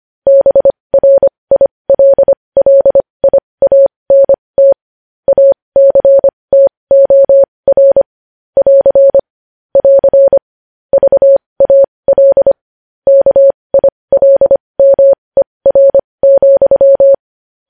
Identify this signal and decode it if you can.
Morse code, heard as a keyed tone, about 25 wpm, spaced as for 20 wpm, 550 Hz. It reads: BRILLIANT ACTOR <AR> <AR> VAL KILMER,
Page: News Headlines in Morse Code at 20 WPM